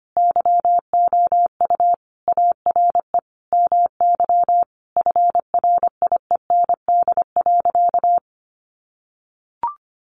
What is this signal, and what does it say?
Morse code, heard as a keyed tone, about 25 wpm, 700 Hz. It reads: YOU ARE MY FRIEND.
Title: Sentences from Top 1000 Words 25wpm